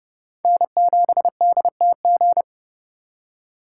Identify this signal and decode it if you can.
Morse code, heard as a keyed tone, about 37 wpm, 700 Hz. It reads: N7DTG
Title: Speed Racing - USA Call Signs 30wpm